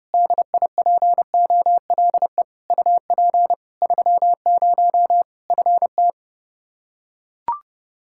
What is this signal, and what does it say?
Morse code, heard as a keyed tone, about 30 wpm, 700 Hz. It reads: DIPOLE UP 30 FT
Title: Practice QSO - Exchange 2 - Info - 30wpm